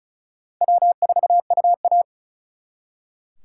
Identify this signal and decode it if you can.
Morse code, heard as a keyed tone, about 44 wpm, 700 Hz. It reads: W4UA